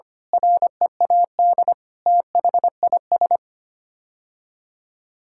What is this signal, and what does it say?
Morse code, heard as a keyed tone, about 25 wpm, 700 Hz. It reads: READ THIS
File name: Sets of 9 Words - 25wpm